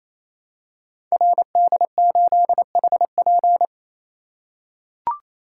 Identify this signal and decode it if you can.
Morse code, heard as a keyed tone, about 27 wpm, 700 Hz. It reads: RD8HP
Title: Code Groups - 5 Letter:Numbers - 28wpm